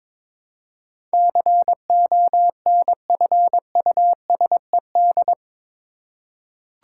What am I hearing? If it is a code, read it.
Morse code, heard as a keyed tone, about 22 wpm, 700 Hz. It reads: CONFUSED
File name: practice / 22wpm / Eight Letter Words - 22wpm